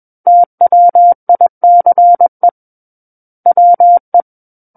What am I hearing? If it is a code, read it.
Morse code, heard as a keyed tone, about 21 wpm, 700 Hz. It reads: TWICE WE
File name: Book One / 12-Chapter 12 - 21 WPM